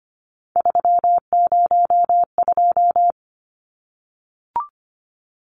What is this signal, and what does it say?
Morse code, heard as a keyed tone, about 25 wpm, 700 Hz. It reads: 302
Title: Numbers - 3-Digit 25wpm